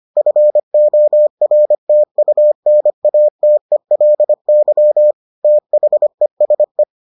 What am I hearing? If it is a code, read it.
Morse code, heard as a keyed tone, about 25 wpm, 600 Hz. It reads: FORTUNATELY THESE